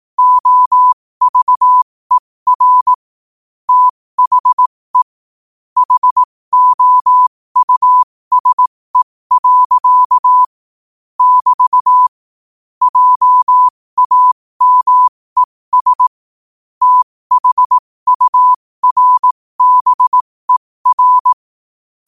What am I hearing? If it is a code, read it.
Morse code, heard as a keyed tone, about 18 wpm, spaced as for 15 wpm, 1000 Hz. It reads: OVER THE HOUSE. <BT> JAMES THURBER